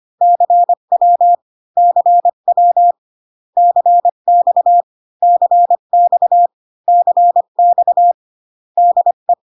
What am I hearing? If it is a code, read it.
Morse code, heard as a keyed tone, about 25 wpm, 700 Hz. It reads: CW CW CX CX CX DE